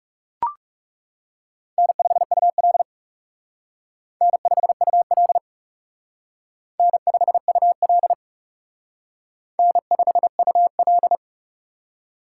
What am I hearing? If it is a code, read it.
Morse code, heard as a keyed tone, about 36 wpm, 700 Hz. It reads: N5UL N5UL N5UL N5UL
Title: Speed Racing - USA 1x2 Call Signs 30wpm